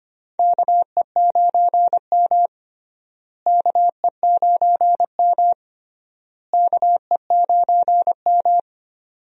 Morse code, heard as a keyed tone, about 25 wpm, 700 Hz. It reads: KE9M KE9M KE9M
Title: USA Call Signs - 4 Characters - Repeated 3x - 25wpm